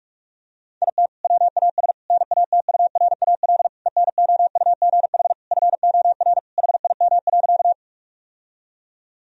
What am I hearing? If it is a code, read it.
Morse code, heard as a keyed tone, about 46 wpm, 700 Hz. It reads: IT WAS NATURAL ENOUGH FOR HIM.
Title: Sentences from Top 1000 Words - Review Part 4 - 45wpm